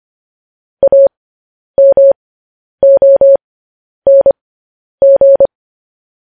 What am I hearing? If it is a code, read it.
Morse code, heard as a keyed tone, about 25 wpm, spaced as for 10 wpm, 550 Hz. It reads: AMONG
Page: news headlines in morse code at 10 wpm